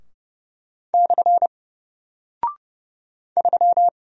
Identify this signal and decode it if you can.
Morse code, heard as a keyed tone, about 30 wpm, 700 Hz. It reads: / 3
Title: Single Character (Letter:Number:Punctuation:Prosign) Alt 30wpm